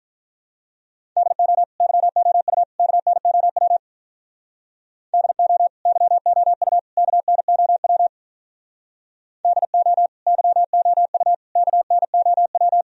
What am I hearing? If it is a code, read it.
Morse code, heard as a keyed tone, about 44 wpm, 700 Hz. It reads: DO YOU KNOW DO YOU KNOW DO YOU KNOW